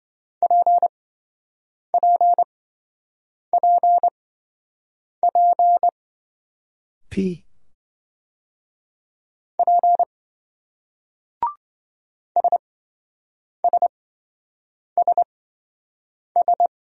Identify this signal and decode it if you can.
Morse code, heard as a keyed tone, about 27 wpm, spaced as for 7 wpm, 700 Hz. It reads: PPPP P SSSS